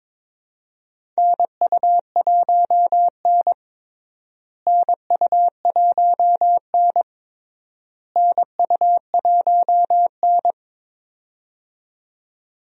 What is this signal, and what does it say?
Morse code, heard as a keyed tone, about 22 wpm, 700 Hz. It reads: NU1N NU1N NU1N
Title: USA Call Signs - 4 Characters - Repeated 3x - 22wpm